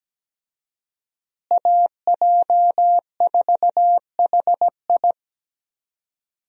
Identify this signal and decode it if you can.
Morse code, heard as a keyed tone, about 17 wpm, 700 Hz. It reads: AJ4HI
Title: USA Call Signs - 5 Characters - Repeated 3x - 17wpm